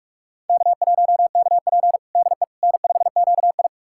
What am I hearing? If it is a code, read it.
Morse code, heard as a keyed tone, about 42 wpm, 700 Hz. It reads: K1KP DE N5QI
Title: USA Callsign DE Callsign - 4 Characters - 45wpm